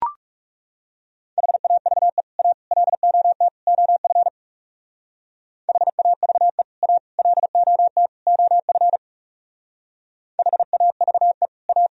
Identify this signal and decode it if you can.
Morse code, heard as a keyed tone, about 36 wpm, 700 Hz. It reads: HAVE A LOT OF HAVE A LOT OF HAVE A
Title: Speed-Racing - Sets of 4 Words 30wpm